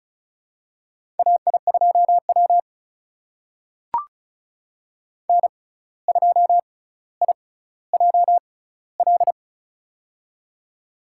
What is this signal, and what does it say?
Morse code, heard as a keyed tone, about 35 wpm, spaced as for 17 wpm, 700 Hz. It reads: AI2W N2IJL
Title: USA Call Signs - 6x Character Spacing - 35wpm